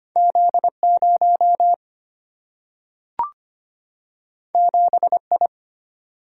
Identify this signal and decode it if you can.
Morse code, heard as a keyed tone, about 25 wpm, 700 Hz. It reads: Z0 7I
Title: Single-Letter Single-Number 25wpm